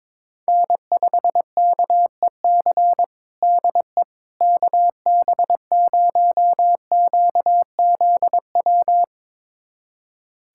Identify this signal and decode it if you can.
Morse code, heard as a keyed tone, about 21 wpm, 700 Hz. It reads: N5KEC DE KB0QZW
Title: USA Callsign DE Callsign 22wpm